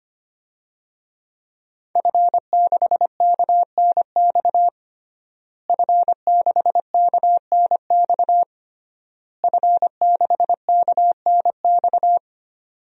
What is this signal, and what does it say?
Morse code, heard as a keyed tone, about 25 wpm, 700 Hz. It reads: F6KNX F6KNX F6KNX
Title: DXCC-100 Call Signs - 5 Characters - Repeated 3x - 25wpm